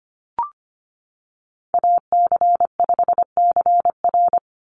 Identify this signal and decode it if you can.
Morse code, heard as a keyed tone, about 25 wpm, 700 Hz. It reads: AC5CR
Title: USA Call Signs 25wpm